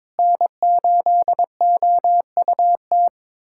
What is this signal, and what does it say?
Morse code, heard as a keyed tone, about 22 wpm, 700 Hz. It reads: N8OUT
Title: USA Callsign DE Callsign - 5 Characters - 22wpm